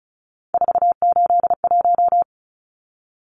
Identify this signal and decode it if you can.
Morse code, heard as a keyed tone, about 34 wpm, 700 Hz. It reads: 481
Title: Numbers - 3-Digit 35wpm